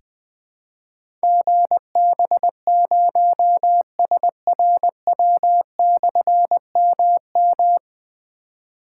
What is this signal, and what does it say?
Morse code, heard as a keyed tone, about 21 wpm, 700 Hz. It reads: GB0SRW/MM
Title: Difficult Call Signs 20wpm